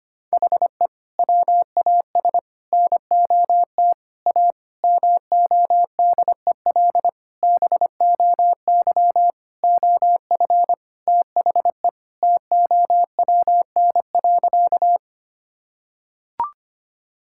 Morse code, heard as a keyed tone, about 25 wpm, 700 Hz. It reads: HE WAS NOT A MODEL BOY OF THE TOWN.
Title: Sentences from Top 1000 Words - Review Part 5 - 25wpm